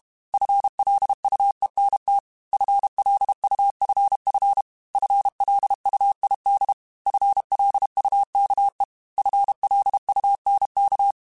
Morse code, heard as a keyed tone, about 32 wpm, 800 Hz. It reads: FLUENT FLUFF FLUID FLUKE FLUNK
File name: FL Runs / FL Runs mp3s / FL Runs @30